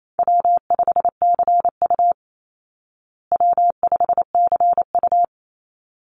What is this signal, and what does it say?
Morse code, heard as a keyed tone, about 28 wpm, 700 Hz. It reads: W5CU W5CU
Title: Call Signs Repeated 3x - 28wpm